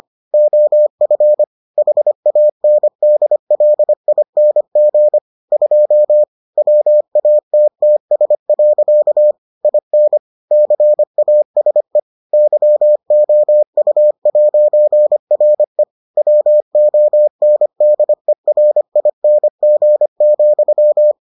Morse code, heard as a keyed tone, about 25 wpm, 600 Hz. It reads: OF HANDLING 2 WATTS. IN CASE YOU'RE WONDERING,